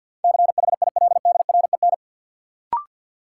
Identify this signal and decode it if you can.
Morse code, heard as a keyed tone, about 50 wpm, 700 Hz. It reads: CHILDREN
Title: Sentences from Top 300 Words - New Word Drill - 50wpm